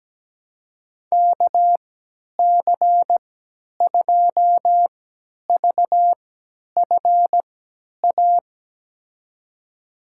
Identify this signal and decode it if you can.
Morse code, heard as a keyed tone, about 17 wpm, spaced as for 10 wpm, 700 Hz. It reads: KC2VFA
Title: USA Call Signs - 6 Characters - 3x Character Spacing - 17wpm